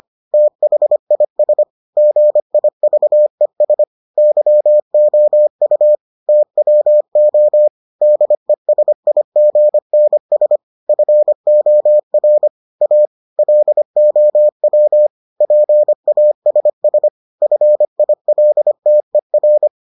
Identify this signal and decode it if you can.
Morse code, heard as a keyed tone, about 25 wpm, 600 Hz. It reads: THIS GIVES YOU TWO DESIGNS FOR A LOW PASS FILTER